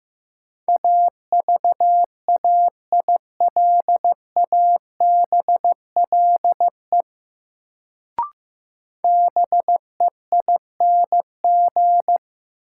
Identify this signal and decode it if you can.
Morse code, heard as a keyed tone, about 15 wpm, 700 Hz. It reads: AVAILABLE BEING
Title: Sentences from Top 600 Words - New Word Drill - 15wpm